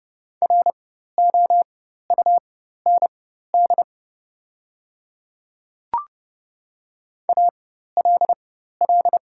Morse code, heard as a keyed tone, about 30 wpm, spaced as for 14 wpm, 690 Hz. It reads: ROUND ALL
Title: 4x Character Spacing - Top 200 Words - 30wpm